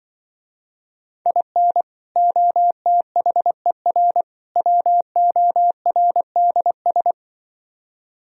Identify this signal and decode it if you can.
Morse code, heard as a keyed tone, about 24 wpm, 700 Hz. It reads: IN OTHER WORDS
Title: Speed-Racing - Sets of 3 Words 20wpm